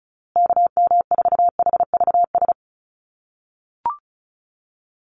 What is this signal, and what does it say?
Morse code, heard as a keyed tone, about 34 wpm, 700 Hz. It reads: KM4HVS
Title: USA Call Signs Repeated 2x 35wpm